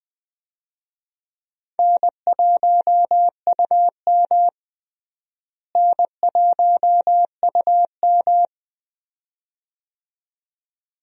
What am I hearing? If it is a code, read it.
Morse code, heard as a keyed tone, about 20 wpm, 700 Hz. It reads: N1UM N1UM
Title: USA Call Signs - 4 Characters - Repeated 2x - 20wpm